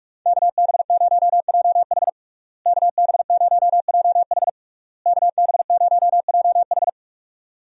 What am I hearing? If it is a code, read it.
Morse code, heard as a keyed tone, about 46 wpm, 700 Hz. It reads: KB0JH KB0JH KB0JH
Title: USA Call Signs - 5 Characters - Repeated 3x - 45wpm